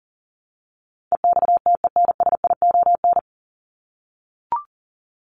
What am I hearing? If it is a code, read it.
Morse code, heard as a keyed tone, about 40 wpm, 700 Hz. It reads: EXTENSION
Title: Top 1501-1750 Words 40wpm